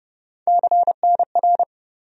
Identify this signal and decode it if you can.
Morse code, heard as a keyed tone, about 30 wpm, 700 Hz. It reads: CNR
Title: USA Call Sign Postfixes - 3 Letters 30wpm